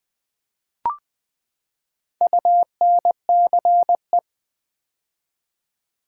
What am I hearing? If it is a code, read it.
Morse code, heard as a keyed tone, about 20 wpm, 700 Hz. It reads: UNCE
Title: Build A Word - 4 Letters At A Time - 11-Letter Words - 20wpm